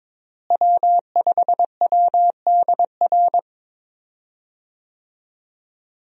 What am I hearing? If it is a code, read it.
Morse code, heard as a keyed tone, about 22 wpm, 700 Hz. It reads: W5WDR
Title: USA Call Signs - 5 Characters - Repeated 2x - 22wpm